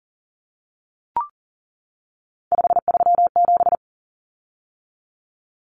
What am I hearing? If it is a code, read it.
Morse code, heard as a keyed tone, about 39 wpm, 700 Hz. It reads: 537
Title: Common Cut Numbers - 3-Digit 40wpm